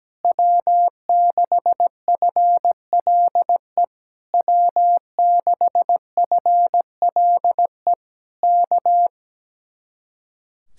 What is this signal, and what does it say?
Morse code, heard as a keyed tone, about 17 wpm, 700 Hz. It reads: W6FLE W6FLE K
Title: Speed-Racing Common QSO Phrases Encore - 17wpm